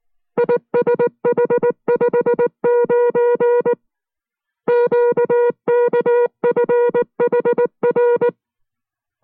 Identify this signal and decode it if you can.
Morse code, heard as a keyed tone, about 19 wpm, 480 Hz. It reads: ISH59 QKFHR